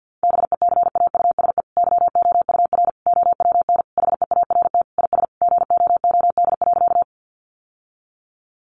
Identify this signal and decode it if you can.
Morse code, heard as a keyed tone, about 50 wpm, 700 Hz. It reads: BECAUSE YOUR OWN HEART IS GOOD.
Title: Sentences from Top 1200 Words 50wpm